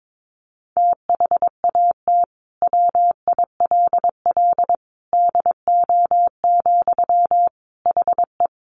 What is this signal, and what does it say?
Morse code, heard as a keyed tone, about 22 wpm, 700 Hz. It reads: THAT WILL DO, HE
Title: Sentences from Top 100 Words 22wpm